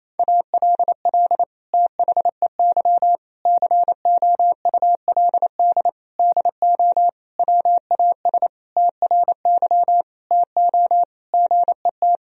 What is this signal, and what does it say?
Morse code, heard as a keyed tone, about 28 wpm, 700 Hz. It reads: ALL THEY COULD DO WAS TRY TO GET
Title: Sentences from Top 1000 Words - Review Part 1 - 28wpm